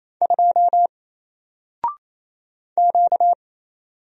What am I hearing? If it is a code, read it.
Morse code, heard as a keyed tone, about 28 wpm, 700 Hz. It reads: 2 Q
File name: Single-Letter-Number-28wpm